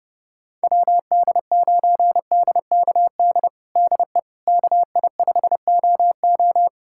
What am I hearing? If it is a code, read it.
Morse code, heard as a keyed tone, about 30 wpm, 700 Hz. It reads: WD9DKD DE KI5OO
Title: USA Callsign DE Callsign 30wpm